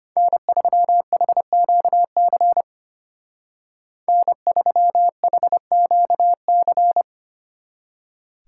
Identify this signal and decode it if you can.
Morse code, heard as a keyed tone, about 31 wpm, 700 Hz. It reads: N3HQC N3HQC
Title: Speed Racing - USA 1x3 Call Signs 25wpm